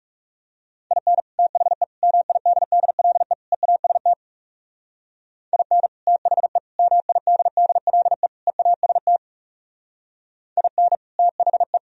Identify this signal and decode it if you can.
Morse code, heard as a keyed tone, about 37 wpm, 700 Hz. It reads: IN THE MIDDLE EAST IN THE MIDDLE EAST IN THE
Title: Speed-Racing - Sets of 4 Words 30wpm